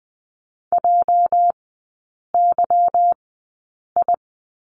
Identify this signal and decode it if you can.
Morse code, heard as a keyed tone, about 19 wpm, spaced as for 8 wpm, 690 Hz. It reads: JYI
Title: 205 Lesson 33e - Letter j - Three Characters 20wpm